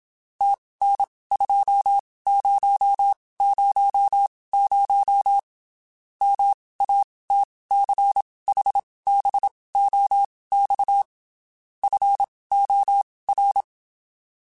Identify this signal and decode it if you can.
Morse code, heard as a keyed tone, about 27 wpm, spaced as for 19 wpm, 800 Hz. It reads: TN2000 MATCHBOX FOR
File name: (spaced) / QSOs Spaced-mp3s / WD0FBT 1 @25 sp